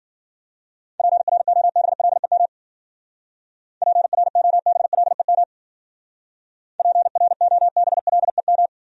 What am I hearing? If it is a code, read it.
Morse code, heard as a keyed tone, about 51 wpm, 700 Hz. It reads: PROBLEM PROBLEM PROBLEM